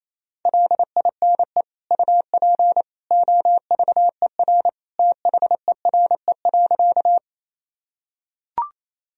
Morse code, heard as a keyed tone, about 28 wpm, 700 Hz. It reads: LINE UP OVER THERE.
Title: Sentences from Top 1000 Words - Review Part 1 - 28wpm